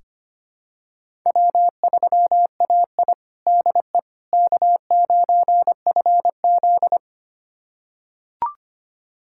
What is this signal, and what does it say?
Morse code, heard as a keyed tone, about 25 wpm, 700 Hz. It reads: W3AI DE K9FZ
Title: USA Callsign DE Callsign - 4 Characters - 25wpm